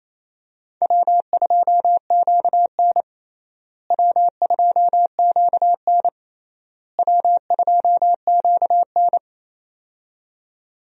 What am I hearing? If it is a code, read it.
Morse code, heard as a keyed tone, about 27 wpm, 700 Hz. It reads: W2QN W2QN W2QN
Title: USA Call Signs - 4 Characters - Repeated 3x - 28wpm